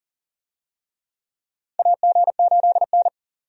Morse code, heard as a keyed tone, about 37 wpm, 700 Hz. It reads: AG8N
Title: USA Call Signs - 4 Characters - 40wpm